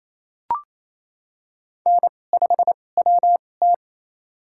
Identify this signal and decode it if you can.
Morse code, heard as a keyed tone, about 28 wpm, spaced as for 20 wpm, 700 Hz. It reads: N5WT
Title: USA Call Signs - 4 Characters - 2x Character Spacing - 28wpm